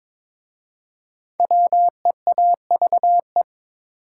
Morse code, heard as a keyed tone, about 22 wpm, 700 Hz. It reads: WEAVE
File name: 5 Letter Words - First 3 Letters Given - 22wpm